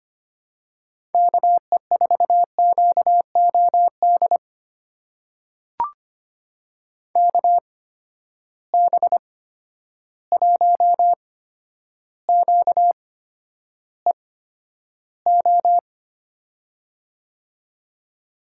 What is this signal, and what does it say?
Morse code, heard as a keyed tone, about 25 wpm, spaced as for 11 wpm, 700 Hz. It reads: KE4QOD KB1QEO
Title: USA Call Signs - 6 Characters - 8x Character Spacing - 25wpm